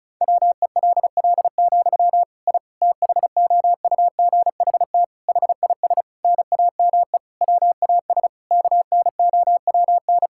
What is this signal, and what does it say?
Morse code, heard as a keyed tone, about 34 wpm, 700 Hz. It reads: WELL, I THOUGHT HIS NAME WAS KNOWN
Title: Sentences from Top 1000 Words - Review Part 5 - 35wpm